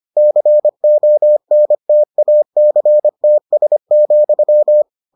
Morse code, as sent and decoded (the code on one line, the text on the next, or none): -.-. --- -. - .- -.-. - ... --..--
CONTACTS,